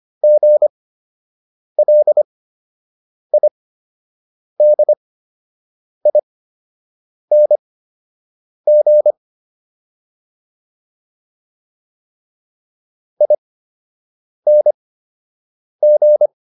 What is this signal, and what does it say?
Morse code, heard as a keyed tone, about 25 wpm, spaced as for 7 wpm, 600 Hz. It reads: GLIDING ING